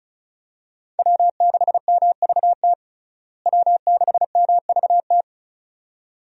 Morse code, heard as a keyed tone, about 36 wpm, 700 Hz. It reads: W6MVT W6MVT
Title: USA Call Signs - 5 Characters - Repeated 2x - 35wpm